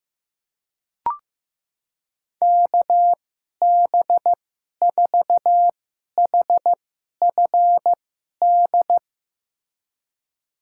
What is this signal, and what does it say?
Morse code, heard as a keyed tone, about 16 wpm, spaced as for 11 wpm, 700 Hz. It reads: KB4HFD
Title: USA Call Signs - 6 Characters - 2x Character Spacing - 15wpm